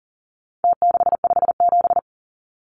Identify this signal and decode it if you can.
Morse code, heard as a keyed tone, about 40 wpm, 700 Hz. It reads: T657
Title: Numbers - Contest Style - 4-Digit 40wpm